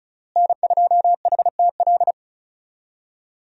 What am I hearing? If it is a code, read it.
Morse code, heard as a keyed tone, about 34 wpm, 700 Hz. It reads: N2HTL
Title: USA Call Signs - 5 Characters - Repeated 3x - 35wpm